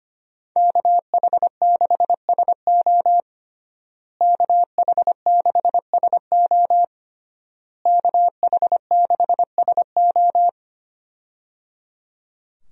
Morse code, heard as a keyed tone, about 25 wpm, 700 Hz. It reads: KH6SO KH6SO KH6SO
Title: USA Call Signs - 5 Characters - Repeated 3x - 25wpm